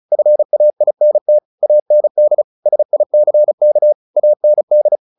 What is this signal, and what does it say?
Morse code, heard as a keyed tone, about 34 wpm, 600 Hz. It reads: FAINT AND SICK AND
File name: the old man and the sea 2 35wpm